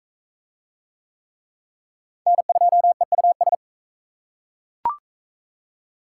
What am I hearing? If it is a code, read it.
Morse code, heard as a keyed tone, about 41 wpm, 700 Hz. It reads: N2EUS